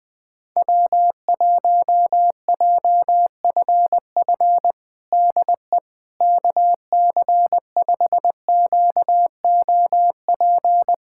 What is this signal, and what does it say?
Morse code, heard as a keyed tone, about 20 wpm, 700 Hz. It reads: W1JFF DE KC5QOP
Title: USA Callsign DE Callsign 20wpm